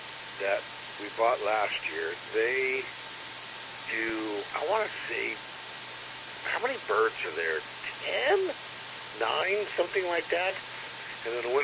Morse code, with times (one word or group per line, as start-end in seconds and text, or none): none